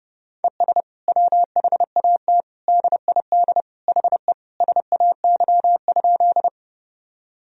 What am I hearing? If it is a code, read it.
Morse code, heard as a keyed tone, about 32 wpm, 700 Hz. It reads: ES WHAT DID HE SAY?
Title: Sentences from Top 200 Words and Abbreviations 30wpm